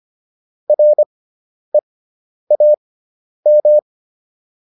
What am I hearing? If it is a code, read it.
Morse code, heard as a keyed tone, about 25 wpm, spaced as for 10 wpm, 600 Hz. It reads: REAM